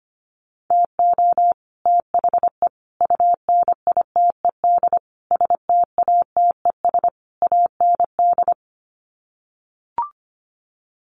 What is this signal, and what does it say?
Morse code, heard as a keyed tone, about 25 wpm, 700 Hz. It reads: TO THE UNITED STATES AND
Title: Sets of 5 Words Encore 25wpm